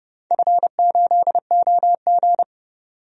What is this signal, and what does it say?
Morse code, heard as a keyed tone, about 30 wpm, 700 Hz. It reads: F8OG
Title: DXCC-100 Call Signs - 4 Characters - 30wpm